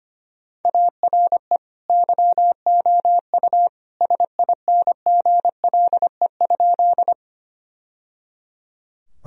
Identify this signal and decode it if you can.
Morse code, heard as a keyed tone, about 25 wpm, 700 Hz. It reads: ARE YOU SINGLE?
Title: Sentences from Top 1000 Words - Review Part 4 - 25wpm